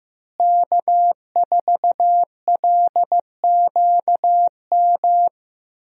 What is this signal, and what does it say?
Morse code, heard as a keyed tone, about 15 wpm, 700 Hz. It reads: K4LQM
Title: USA Call Signs - 5 Characters - Repeated 3x - 15wpm